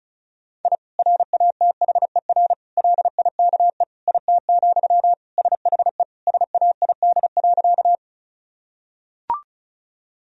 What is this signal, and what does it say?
Morse code, heard as a keyed tone, about 34 wpm, 700 Hz. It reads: I RATHER LIKE IT, SHE SAID.
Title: Sentences from Top 1000 Words - Review Part 4 - 35wpm